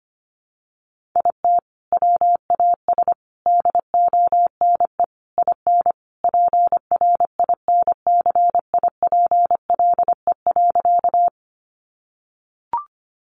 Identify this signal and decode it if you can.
Morse code, heard as a keyed tone, about 25 wpm, 700 Hz. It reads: IT WAS DONE IN PRINCIPLE.
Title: Sentences from Top 1400 Words 25wpm